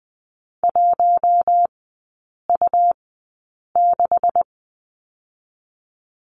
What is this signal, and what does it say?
Morse code, heard as a keyed tone, about 20 wpm, spaced as for 8 wpm, 690 Hz. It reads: 1U6
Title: 166 Lesson 27e - Number 7 - Three Characters 20wpm